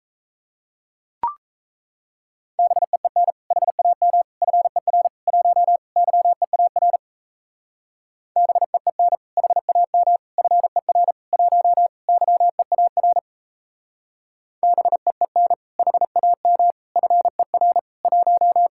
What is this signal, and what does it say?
Morse code, heard as a keyed tone, about 36 wpm, 700 Hz. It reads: BEEN HAM FER 1 YEAR BEEN HAM FER 1 YEAR BEEN HAM FER 1